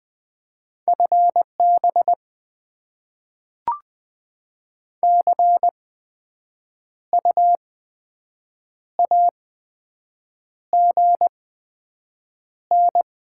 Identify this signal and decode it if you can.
Morse code, heard as a keyed tone, about 20 wpm, spaced as for 7 wpm, 700 Hz. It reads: FB CU AGN
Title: Top 20 QSO Elements - 8x Character Spacing - 20wpm